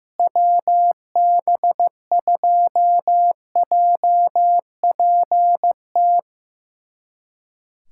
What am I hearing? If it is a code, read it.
Morse code, heard as a keyed tone, about 15 wpm, 700 Hz. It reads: WB2JPT
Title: USA Callsign DE Callsign 15wpm